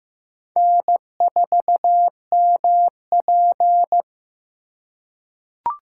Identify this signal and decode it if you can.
Morse code, heard as a keyed tone, about 15 wpm, 700 Hz. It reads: N4MP
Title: USA Call Signs - 4 Characters - 3x Character Spacing - 15wpm